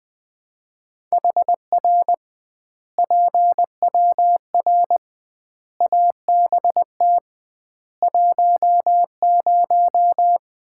Morse code, heard as a keyed tone, about 20 wpm, 700 Hz. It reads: HR PWR ABT 10
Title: Common QSO Phrases - 2x Word-Spacing - 20wpm